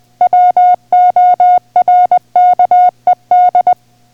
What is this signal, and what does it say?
Morse code, heard as a keyed tone, about 20 wpm, 720 Hz. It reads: WORKED